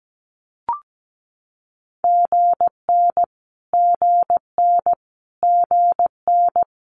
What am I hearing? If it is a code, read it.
Morse code, heard as a keyed tone, about 17 wpm, 700 Hz. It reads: GN GN GN
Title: Multi-letter Phonemes Repeated 3x 17wpm